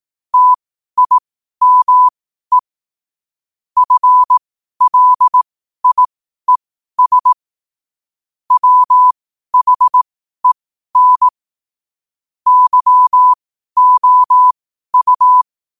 Morse code, heard as a keyed tone, about 18 wpm, spaced as for 12 wpm, 1000 Hz. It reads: TIME FLIES WHEN YOU
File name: qotdfiles / 12wpm